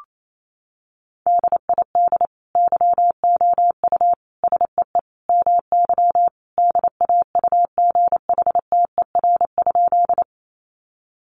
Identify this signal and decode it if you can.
Morse code, heard as a keyed tone, about 28 wpm, 700 Hz. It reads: DID YOU SEE MY DAUGHTER?
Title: Sentences from Top 1500 Words 28wpm